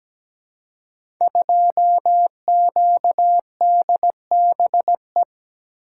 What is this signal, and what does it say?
Morse code, heard as a keyed tone, about 17 wpm, 700 Hz. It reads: 2QDBE
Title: Code Groups - 5 Letter:Numbers - 17wpm